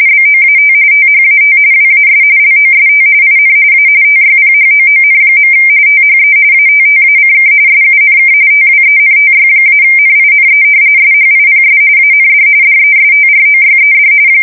RTTY, about 45.45 baud. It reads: BOAT THAT WAS LOST IN A TYPHOON. THE ATTACK SUNK TWENTY-NINE SHIPS AND DESTROYED
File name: fsk170 / battleshipnj